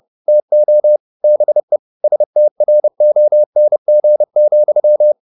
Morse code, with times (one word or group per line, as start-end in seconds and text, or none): 0.3-1.0 TO
1.2-1.8 BE
2.0-5.1 STRONG,